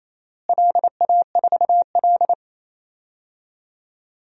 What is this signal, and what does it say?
Morse code, heard as a keyed tone, about 28 wpm, 700 Hz. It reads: LA4L
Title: DXCC-100 Call Signs - 4 Characters - Repeated 3x - 28wpm